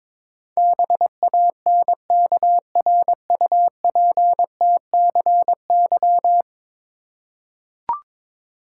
Morse code, heard as a keyed tone, about 21 wpm, 700 Hz. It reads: BANKRUPTCY
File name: practice / 22wpm / 10 Letter Words - First 3 Letters Given - 22wpm